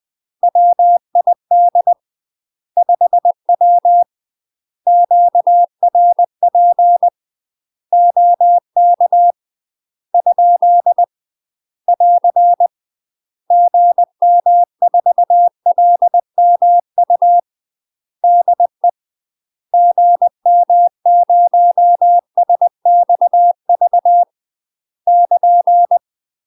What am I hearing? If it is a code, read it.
Morse code, heard as a keyed tone, about 20 wpm, 700 Hz. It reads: WID 5W QRP OK ? <AR> GM4LMU DE GM0SXV <KN>
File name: UK MASTER QSOs TRAINING/20WPM